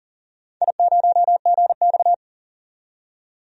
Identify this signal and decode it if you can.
Morse code, heard as a keyed tone, about 40 wpm, 700 Hz. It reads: I0GX
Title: DXCC-100 Call Signs - 4 Characters - 40wpm